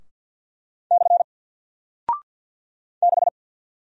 Morse code, heard as a keyed tone, about 48 wpm, 700 Hz. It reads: / 6
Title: Single Character (Letter:Number:Punctuation:Prosign) Alt 50wpm